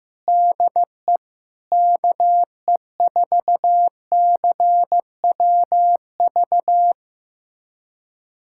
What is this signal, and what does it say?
Morse code, heard as a keyed tone, about 16 wpm, 700 Hz. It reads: DE KE4CWV
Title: USA Callsign DE Callsign 15wpm